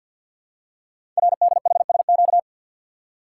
Morse code, heard as a keyed tone, about 48 wpm, 700 Hz. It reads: RDHSQ
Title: Code Groups - 5 Letters - 50wpm